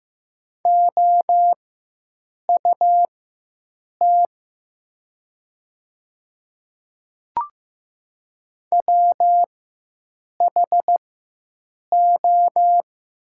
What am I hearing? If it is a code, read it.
Morse code, heard as a keyed tone, about 15 wpm, spaced as for 7 wpm, 700 Hz. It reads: OUT WHO
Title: 4x Character Spacing - Top 200 Words - 15wpm